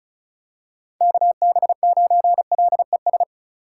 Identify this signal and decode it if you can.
Morse code, heard as a keyed tone, about 36 wpm, 700 Hz. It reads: KB9LES
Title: USA Call Signs - 6 Characters - 2x Character Spacing - 35wpm